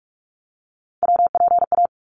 Morse code, heard as a keyed tone, about 44 wpm, 700 Hz. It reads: WPA
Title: ARRL Sections 45wpm